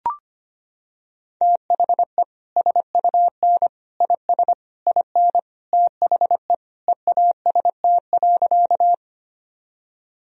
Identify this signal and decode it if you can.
Morse code, heard as a keyed tone, about 25 wpm, 700 Hz. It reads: THE SUN IS IN THE EAST.
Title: Sentences from Top 1000 Words - Review Part 3 - 25wpm